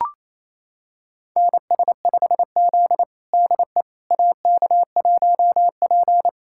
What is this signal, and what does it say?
Morse code, heard as a keyed tone, about 28 wpm, 700 Hz. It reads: NS5Z DE AK1P
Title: USA Callsign DE Callsign - 4 Characters - 28wpm